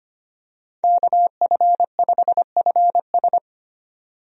Morse code, heard as a keyed tone, about 25 wpm, 700 Hz. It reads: KF5FS
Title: USA Call Signs - 5 Characters - 25wpm